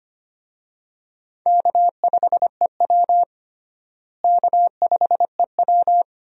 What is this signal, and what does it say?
Morse code, heard as a keyed tone, about 25 wpm, 700 Hz. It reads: K5EW K5EW
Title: USA Call Signs - 4 Characters - Repeated 3x - 25wpm